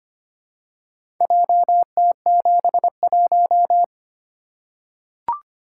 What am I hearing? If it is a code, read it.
Morse code, heard as a keyed tone, about 25 wpm, 700 Hz. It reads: JT71
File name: practice / 25wpm / Four Letter-Number Combinations - 25wpm